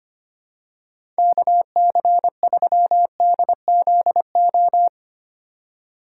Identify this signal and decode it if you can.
Morse code, heard as a keyed tone, about 25 wpm, 700 Hz. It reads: KC3DZO
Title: USA Call Signs - 2x Character Spacing - 25wpm